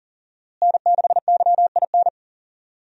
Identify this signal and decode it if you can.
Morse code, heard as a keyed tone, about 37 wpm, 700 Hz. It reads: N6YIN
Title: USA Call Signs - 5 Characters - Repeated 2x - 40wpm